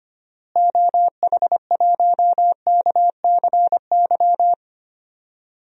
Morse code, heard as a keyed tone, about 25 wpm, 700 Hz. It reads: OH1KCY
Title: DXCC-100 Call Signs - 6 Characters - Repeated 3x - 25wpm